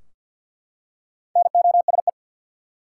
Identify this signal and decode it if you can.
Morse code, heard as a keyed tone, about 48 wpm, 700 Hz. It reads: NOSE
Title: Four Letter Words - 50wpm